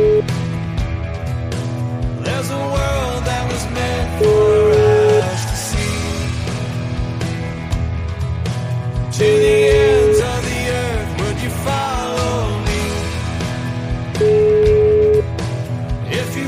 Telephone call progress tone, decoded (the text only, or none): ringback tone